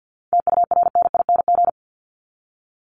Morse code, heard as a keyed tone, about 50 wpm, 700 Hz. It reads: TURNING
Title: ING Suffix - Top 100 Words with Base Words 50wpm